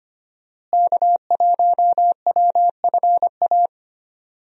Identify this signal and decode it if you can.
Morse code, heard as a keyed tone, about 25 wpm, 700 Hz. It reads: K1WFA
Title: USA Call Signs - 5 Characters - Repeated 2x - 25wpm